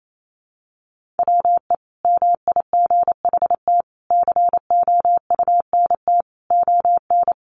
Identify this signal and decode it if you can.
Morse code, heard as a keyed tone, about 28 wpm, 700 Hz. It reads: WE MIGHT COUNT ON
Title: Sentences from Top 1500 Words 28wpm